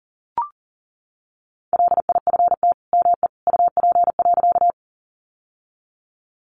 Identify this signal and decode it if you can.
Morse code, heard as a keyed tone, about 37 wpm, 700 Hz. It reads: LIFT ME UP.
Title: Sentences from Top 1000 Words 40wpm